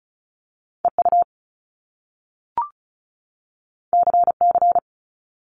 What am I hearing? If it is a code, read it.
Morse code, heard as a keyed tone, about 36 wpm, 700 Hz. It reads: EU CC
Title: Top Two Letter Combinations in Words 35wpm